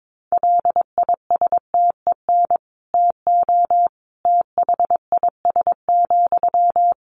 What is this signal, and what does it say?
Morse code, heard as a keyed tone, about 22 wpm, 700 Hz. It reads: LISTEN TO THIS,
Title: Sentences from Top 400 Words 22wpm